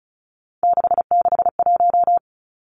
Morse code, heard as a keyed tone, about 36 wpm, 700 Hz. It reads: 661
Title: Three Challenging Letter-Number Combinations 35wpm